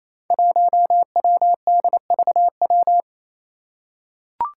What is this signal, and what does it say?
Morse code, heard as a keyed tone, about 28 wpm, 700 Hz. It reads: 1WDVW
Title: Five Challenging Letter-Number Combinations 28wpm